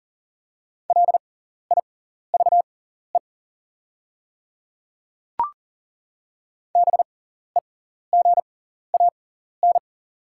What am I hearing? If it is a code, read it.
Morse code, heard as a keyed tone, about 40 wpm, spaced as for 14 wpm, 690 Hz. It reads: LIVE BEGAN
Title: 6x Character Spacing - Top 500 Words - 40wpm